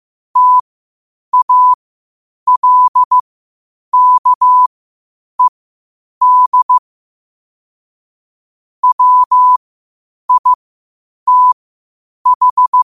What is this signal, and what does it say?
Morse code, heard as a keyed tone, about 15 wpm, spaced as for 8 wpm, 1000 Hz. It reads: TALKED WITH
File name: qotdfiles / 8wpm